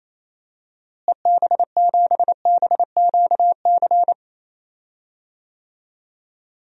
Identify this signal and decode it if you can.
Morse code, heard as a keyed tone, about 28 wpm, 700 Hz. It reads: EB7BQC
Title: DXCC-100 Call Signs - 6 Characters - Repeated 2x - 28wpm